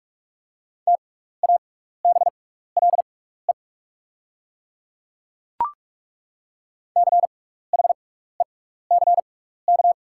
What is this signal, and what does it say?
Morse code, heard as a keyed tone, about 42 wpm, spaced as for 15 wpm, 700 Hz. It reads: TABLE CHECK
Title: 6x Character Spacing - Top 500 Words - 45wpm